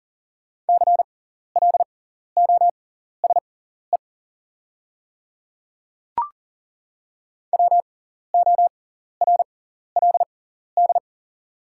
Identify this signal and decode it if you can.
Morse code, heard as a keyed tone, about 37 wpm, spaced as for 14 wpm, 700 Hz. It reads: CLOSE WORLD
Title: 6x Character Spacing - Top 300 Words - 40wpm